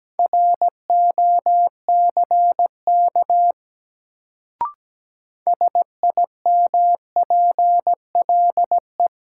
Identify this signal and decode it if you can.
Morse code, heard as a keyed tone, about 17 wpm, 700 Hz. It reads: ROCK SIMPLE
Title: Sentences from Top 400 Words - New Word Drill - 17wpm